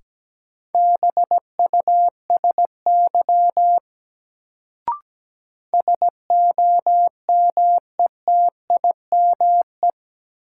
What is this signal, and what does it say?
Morse code, heard as a keyed tone, about 17 wpm, 700 Hz. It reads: BUSY SOMETIME
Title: Sentences from Top 1500 Words - New Word Drill Alt - 17wpm